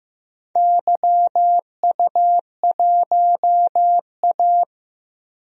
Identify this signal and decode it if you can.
Morse code, heard as a keyed tone, about 15 wpm, 700 Hz. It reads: YU1A
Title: DXCC-100 Call Signs - 4 Characters - 15wpm